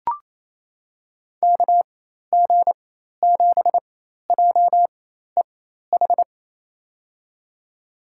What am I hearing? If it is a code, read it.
Morse code, heard as a keyed tone, about 28 wpm, spaced as for 13 wpm, 700 Hz. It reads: KG7JEH
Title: USA Call Signs - 6 Characters - 4x Character Spacing - 28wpm